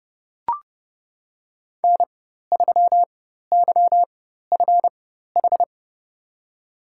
Morse code, heard as a keyed tone, about 32 wpm, spaced as for 14 wpm, 700 Hz. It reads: N3YFH
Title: USA Call Signs - 5 Characters - 4x Character Spacing - 30wpm